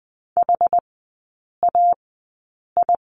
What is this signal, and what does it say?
Morse code, heard as a keyed tone, about 20 wpm, spaced as for 8 wpm, 690 Hz. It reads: HAI
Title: 087 Lesson 15e - Number 5 - Three Characters 20wpm